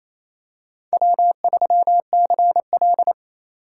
Morse code, heard as a keyed tone, about 27 wpm, 690 Hz. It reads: W3CL
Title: USA Call Signs - 4 Characters - 2x Character Spacing - 28wpm